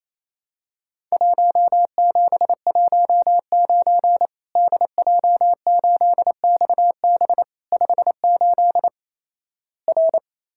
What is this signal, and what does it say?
Morse code, heard as a keyed tone, about 28 wpm, 700 Hz. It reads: 1719 DJ8XB 58 R
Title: Worked All Europe DX Contest - QTC Exchange - 28wpm